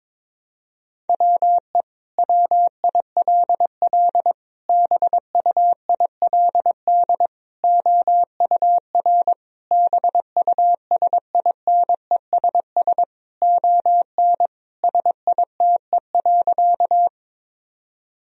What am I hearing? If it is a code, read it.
Morse code, heard as a keyed tone, about 22 wpm, 700 Hz. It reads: WE WILL BUILD OUR BUSINESS ON SITE.
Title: Sentences from Top 1000 Words - Review Part 4 - 22wpm